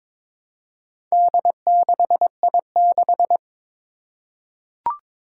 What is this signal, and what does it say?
Morse code, heard as a keyed tone, about 21 wpm, 700 Hz. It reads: D6I6
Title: Four Challenging Letter-Number Combinations 22wpm